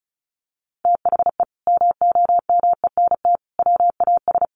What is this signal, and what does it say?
Morse code, heard as a keyed tone, about 34 wpm, 700 Hz. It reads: THE MOMENT WAS
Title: Sentences from Top 1100 Words 35wpm